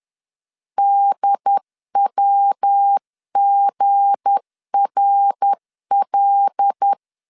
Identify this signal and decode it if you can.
Morse code, heard as a keyed tone, about 10 wpm, 800 Hz. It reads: DWGRL
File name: LEZIONI / 9 286 10wpm 2735